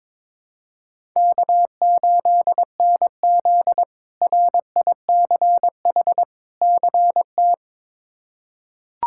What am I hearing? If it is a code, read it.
Morse code, heard as a keyed tone, about 22 wpm, 700 Hz. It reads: K8NZ RICH CT
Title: K1USN SST QSO Practice - 22wpm